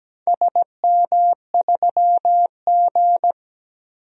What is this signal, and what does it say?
Morse code, heard as a keyed tone, about 17 wpm, 700 Hz. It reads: SM3G
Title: DXCC-100 Call Signs - 4 Characters - 17wpm